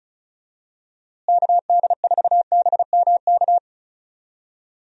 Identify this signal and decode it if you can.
Morse code, heard as a keyed tone, about 34 wpm, 700 Hz. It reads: KD4BMK
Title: USA Call Signs - 6 Characters - Repeated 3x - 35wpm